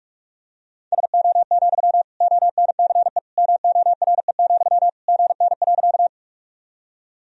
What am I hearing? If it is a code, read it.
Morse code, heard as a keyed tone, about 41 wpm, 700 Hz. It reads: SO, ONCE MORE, GN.
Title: Sentences from Top 300 Words and Abbreviations 45wpm